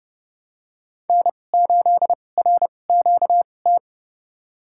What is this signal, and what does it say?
Morse code, heard as a keyed tone, about 29 wpm, spaced as for 21 wpm, 700 Hz. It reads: N8RQT
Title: USA Call Signs - 2x Character Spacing - 30wpm